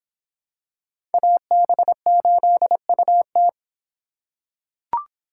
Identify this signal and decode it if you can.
Morse code, heard as a keyed tone, about 26 wpm, 700 Hz. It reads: AB8UT